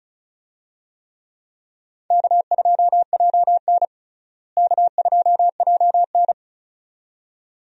Morse code, heard as a keyed tone, about 35 wpm, 700 Hz. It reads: K2JN K2JN
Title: USA Call Signs - 4 Characters - Repeated 2x - 35wpm